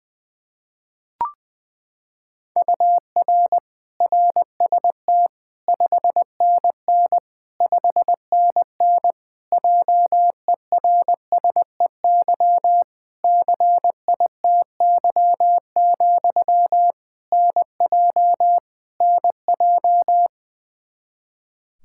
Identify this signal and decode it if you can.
Morse code, heard as a keyed tone, about 20 wpm, 700 Hz. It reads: UR RST 5NN 5NN JERSEY CITY, NJ NJ
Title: Practice QSO - Exchange 1 - Beginning - 20wpm